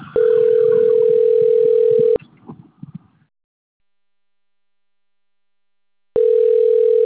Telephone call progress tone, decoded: ringback tone

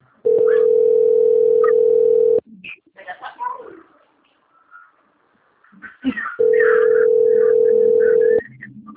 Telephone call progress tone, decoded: ringback tone

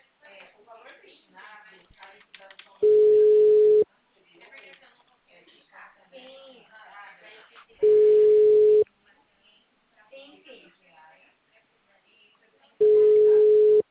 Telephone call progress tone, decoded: ringback tone

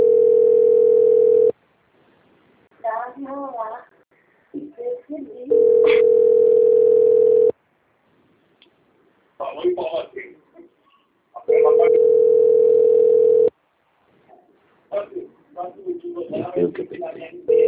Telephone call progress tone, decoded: ringback tone

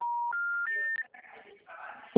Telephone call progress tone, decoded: special information tone

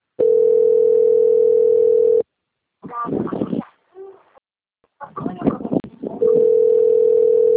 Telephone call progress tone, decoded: ringback tone

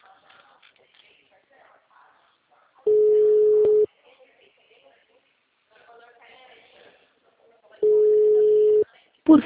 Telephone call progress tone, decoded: ringback tone

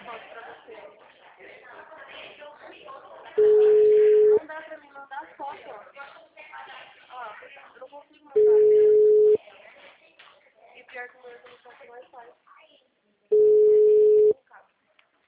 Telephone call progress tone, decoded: ringback tone

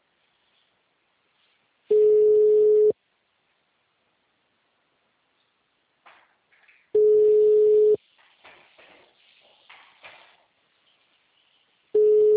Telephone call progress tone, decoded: ringback tone

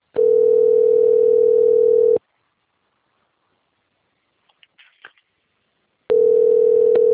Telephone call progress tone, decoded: ringback tone

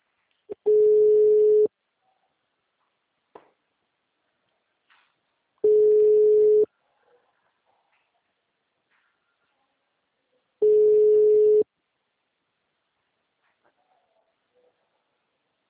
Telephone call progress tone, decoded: ringback tone

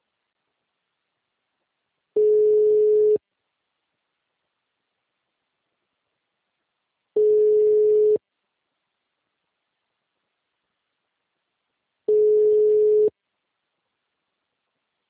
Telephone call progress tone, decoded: ringback tone

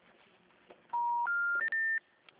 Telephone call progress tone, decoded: special information tone

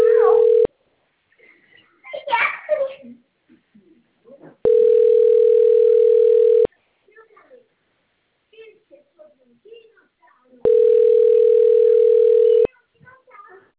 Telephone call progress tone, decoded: ringback tone

